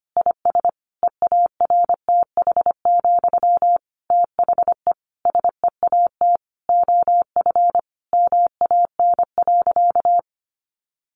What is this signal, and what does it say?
Morse code, heard as a keyed tone, about 25 wpm, 700 Hz. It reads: IS EARTH, THE SEAT OF MAN.
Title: Sentences from Top 1500 Words 25wpm